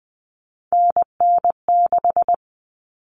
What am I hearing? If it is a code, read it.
Morse code, heard as a keyed tone, about 20 wpm, 700 Hz. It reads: NN6
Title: Common Cut Numbers - 3-Digit 20wpm